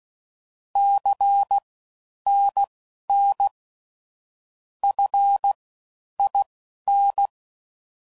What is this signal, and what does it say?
Morse code, heard as a keyed tone, about 16 wpm, spaced as for 10 wpm, 800 Hz. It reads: CNN FIN
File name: Mixed / CW Comparisons / CW Comparisons Broken Characters